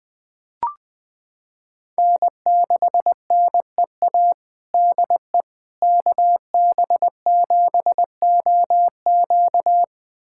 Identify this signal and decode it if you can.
Morse code, heard as a keyed tone, about 20 wpm, 700 Hz. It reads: N6NEA DE KB7OQ
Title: USA Callsign DE Callsign - 5 Characters - 20wpm